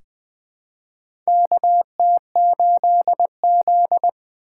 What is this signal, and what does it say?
Morse code, heard as a keyed tone, about 20 wpm, 700 Hz. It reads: KT8Z
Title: USA Call Signs - 4 Characters - 4x Character Spacing - 20wpm